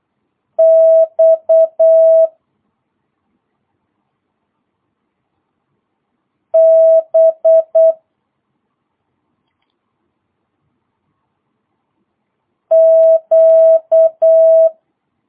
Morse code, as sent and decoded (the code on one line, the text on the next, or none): -..- / -... / --.-
X B Q